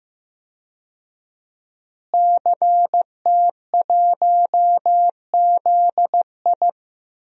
Morse code, heard as a keyed tone, about 15 wpm, 700 Hz. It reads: CT1ZI
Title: DXCC-100 Call Signs - 5 Characters - Repeated 2x - 15wpm